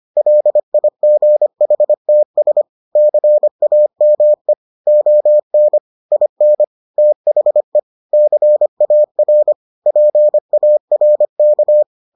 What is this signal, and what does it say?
Morse code, heard as a keyed tone, about 25 wpm, 600 Hz. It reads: LIGHTS CAME ON IN THE CAR PARK